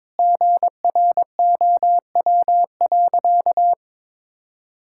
Morse code, heard as a keyed tone, about 22 wpm, 700 Hz. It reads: GROW.
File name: Sentences with 9-Word Phrases - 22wpm